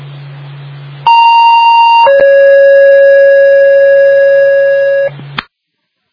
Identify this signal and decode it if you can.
Two-tone page, heard as a one-second tone, then a three-second tone, about 930 then 550 Hz. It is heard from 1.1 to 5.1 s.